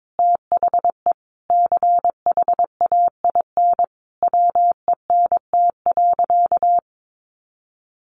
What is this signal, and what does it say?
Morse code, heard as a keyed tone, about 22 wpm, 700 Hz. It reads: THE CHAIN WENT.
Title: Sentences from Top 1500 Words 22wpm